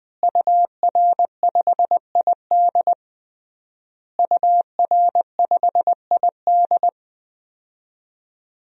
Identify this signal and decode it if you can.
Morse code, heard as a keyed tone, about 20 wpm, 700 Hz. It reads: UR5ID UR5ID
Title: DXCC-100 Call Signs - 5 Characters - Repeated 3x - 20wpm